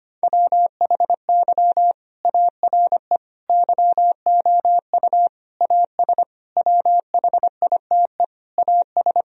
Morse code, heard as a keyed tone, about 25 wpm, 700 Hz. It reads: WHY ARE YOU AS WHITE AS
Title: Sentences from Top 1000 Words - Review Part 3 - 25wpm